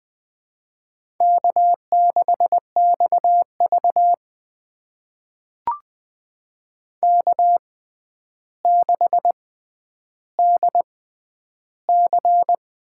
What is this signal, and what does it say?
Morse code, heard as a keyed tone, about 21 wpm, spaced as for 10 wpm, 700 Hz. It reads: K6XV K6DC
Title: USA Call Signs - 6x Character Spacing - 20wpm